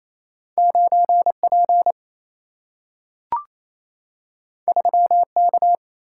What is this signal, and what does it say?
Morse code, heard as a keyed tone, about 28 wpm, 700 Hz. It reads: 9P 3K
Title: Single-Letter Single-Number 28wpm